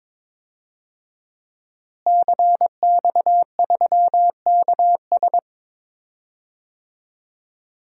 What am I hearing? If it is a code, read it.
Morse code, heard as a keyed tone, about 21 wpm, 700 Hz. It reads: CX3KS
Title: DXCC-100 Call Signs - 5 Characters - Repeated 2x - 22wpm